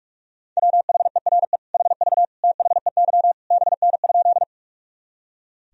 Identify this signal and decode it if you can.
Morse code, heard as a keyed tone, about 46 wpm, 700 Hz. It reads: WHERE HV THEY BN?
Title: Sentences from Top 200 Words and Abbreviations 45wpm